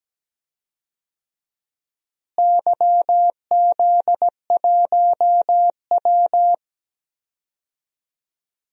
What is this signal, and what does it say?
Morse code, heard as a keyed tone, about 17 wpm, 700 Hz. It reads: YZ1W
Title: DXCC-100 Call Signs - 4 Characters - Repeated 3x - 17wpm